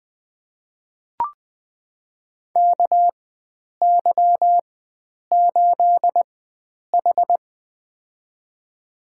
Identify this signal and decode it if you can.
Morse code, heard as a keyed tone, about 20 wpm, spaced as for 9 wpm, 700 Hz. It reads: KY8H
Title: USA Call Signs - 4x Character Spacing - 20wpm